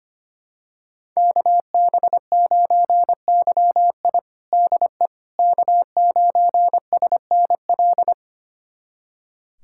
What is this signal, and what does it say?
Morse code, heard as a keyed tone, about 25 wpm, 700 Hz. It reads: KB9YI DE K9SNL
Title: USA Callsign DE Callsign - 5 Characters - 25wpm